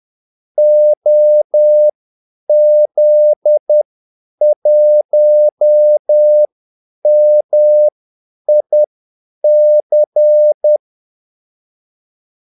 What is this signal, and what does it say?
Morse code, heard as a keyed tone, about 10 wpm, spaced as for 8 wpm, 600 Hz. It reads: OZ1MIC
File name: Callsigns-10-8wpm